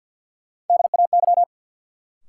Morse code, heard as a keyed tone, about 53 wpm, 700 Hz. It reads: DAY
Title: Sentences from Top 200 Words - New Word Drill - 50wpm